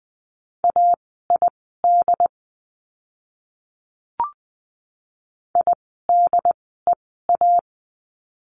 Morse code, heard as a keyed tone, about 20 wpm, spaced as for 14 wpm, 700 Hz. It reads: AID IDEA
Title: 068 Lesson 12f - Letter d - Words 20wpm